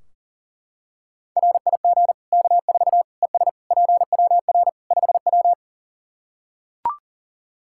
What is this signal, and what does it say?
Morse code, heard as a keyed tone, about 40 wpm, 700 Hz. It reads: RIG K4 ES PWR 5W
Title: K1USN SST QSO Practice 40wpm